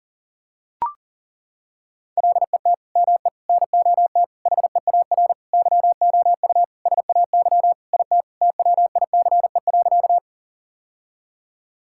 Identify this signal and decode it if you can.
Morse code, heard as a keyed tone, about 40 wpm, 700 Hz. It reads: LET ME NOT HEAR YOU SAY IT TWICE.
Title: Sentences from Top 1600 Words 40wpm